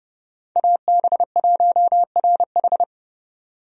Morse code, heard as a keyed tone, about 31 wpm, 700 Hz. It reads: AB1RH
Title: USA Call Signs - 5 Characters - 30wpm